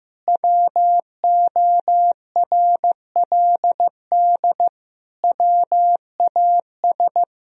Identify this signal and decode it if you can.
Morse code, heard as a keyed tone, about 15 wpm, 700 Hz. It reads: WORLD WAS
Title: Sentences from Top 1000 Words - Review Part 1 - 15wpm